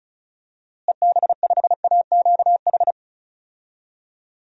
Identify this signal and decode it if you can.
Morse code, heard as a keyed tone, about 36 wpm, 700 Hz. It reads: EB5AQH
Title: DXCC-100 Call Signs - 6 Characters - Repeated 3x - 35wpm